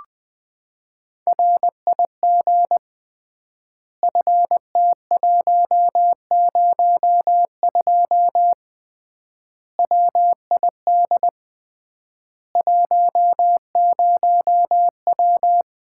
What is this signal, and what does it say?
Morse code, heard as a keyed tone, about 20 wpm, 700 Hz. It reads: RIG FT102 WID 10W
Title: Common QSO Phrases - 3x Word-Spacing - 20wpm